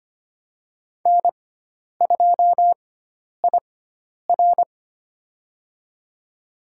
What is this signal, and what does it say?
Morse code, heard as a keyed tone, about 25 wpm, spaced as for 10 wpm, 700 Hz. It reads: N2IR
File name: Lesson 16g - Letter c - Call Signs 25wpm